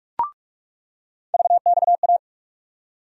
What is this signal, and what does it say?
Morse code, heard as a keyed tone, about 46 wpm, 700 Hz. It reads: VXA
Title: USA Call Sign Postfixes - 3 Letters 45wpm